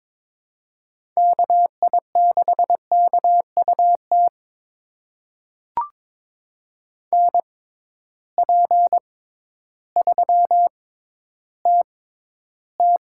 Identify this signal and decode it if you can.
Morse code, heard as a keyed tone, about 22 wpm, spaced as for 12 wpm, 700 Hz. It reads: KI6KUT NP3TT
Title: USA Call Signs - 6x Character Spacing - 22wpm